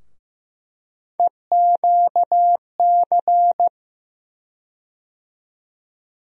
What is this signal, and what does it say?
Morse code, heard as a keyed tone, about 15 wpm, 700 Hz. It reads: EQC
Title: USA Call Sign Postfixes - 3 Letters 15wpm